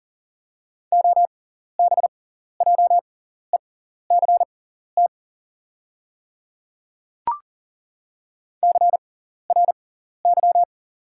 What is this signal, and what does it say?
Morse code, heard as a keyed tone, about 40 wpm, spaced as for 14 wpm, 690 Hz. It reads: OBJECT CRY
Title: 6x Character Spacing - Top 500 Words - 40wpm